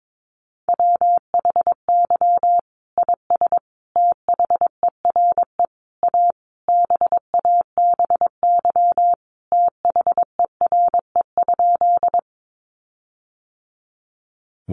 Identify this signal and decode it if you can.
Morse code, heard as a keyed tone, about 22 wpm, 700 Hz. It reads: WHY IS THERE A BABY THERE?
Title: Sentences from Top 1300 Words 22wpm